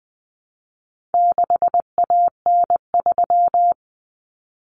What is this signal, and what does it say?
Morse code, heard as a keyed tone, about 20 wpm, 700 Hz. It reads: 6AN3
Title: Common Cut Numbers - 4-Digit 20wpm